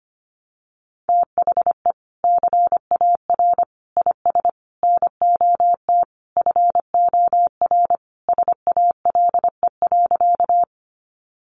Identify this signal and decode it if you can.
Morse code, heard as a keyed tone, about 25 wpm, 700 Hz. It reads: THE CAR IS NOT FOR SALE.
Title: Sentences from Top 1400 Words 25wpm